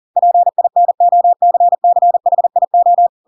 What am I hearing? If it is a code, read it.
Morse code, heard as a keyed tone, about 40 wpm, 700 Hz. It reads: PINOCCHIO
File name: Pinocchio / Pinocchio ch4 40 40 ik1hge